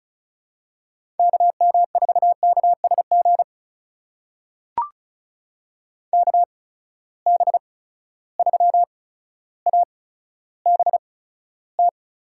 Morse code, heard as a keyed tone, about 36 wpm, spaced as for 15 wpm, 700 Hz. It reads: KM4KSG KB3ABT
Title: USA Call Signs - 6 Characters - 8x Character Spacing - 35wpm